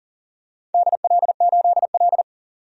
Morse code, heard as a keyed tone, about 41 wpm, 700 Hz. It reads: DL8L
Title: DXCC-100 Call Signs - 4 Characters - 40wpm